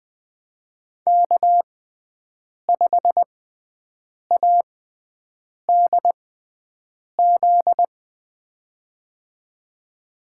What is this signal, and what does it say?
Morse code, heard as a keyed tone, about 21 wpm, spaced as for 7 wpm, 700 Hz. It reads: K5ADZ
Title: USA Call Signs - 5 Characters - 6x Character Spacing - 20wpm